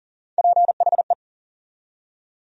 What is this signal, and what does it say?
Morse code, heard as a keyed tone, about 40 wpm, 700 Hz. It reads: PHE
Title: USA Call Sign Postfixes - 3 Letters 40wpm